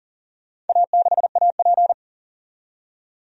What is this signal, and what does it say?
Morse code, heard as a keyed tone, about 40 wpm, 700 Hz. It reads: A6AP
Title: DXCC-100 Call Signs - 4 Characters - 40wpm